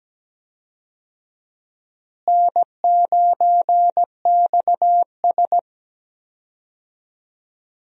Morse code, heard as a keyed tone, about 17 wpm, 700 Hz. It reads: N9XS
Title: USA Call Signs - 4 Characters - Repeated 3x - 17wpm